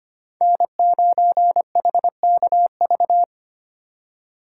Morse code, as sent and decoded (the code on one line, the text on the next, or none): -. ----. .... -.- ...-
N9HKV